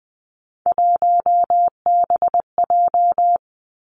Morse code, heard as a keyed tone, about 20 wpm, 700 Hz. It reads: 1BJ
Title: Three Challenging Letter-Number Combinations 20wpm